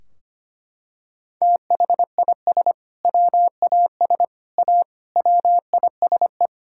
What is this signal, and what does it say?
Morse code, heard as a keyed tone, about 25 wpm, 700 Hz. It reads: THIS WAS A WISE
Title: Sentences from Top 1600 Words 25wpm